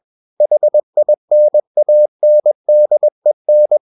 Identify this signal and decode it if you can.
Morse code, heard as a keyed tone, about 21 wpm, 600 Hz. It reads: HINANDEN
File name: Ferie QRS 21wpm